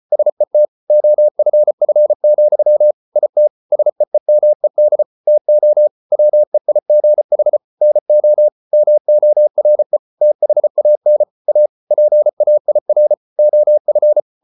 Morse code, heard as a keyed tone, about 34 wpm, 600 Hz. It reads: SET OFF, IT SEEMED TO WEIGH NO MORE THAN A PAIR OF